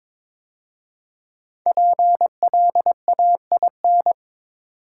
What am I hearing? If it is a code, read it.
Morse code, heard as a keyed tone, about 22 wpm, 700 Hz. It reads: PLAIN
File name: 5 Letter Words - First 3 Letters Given - 22wpm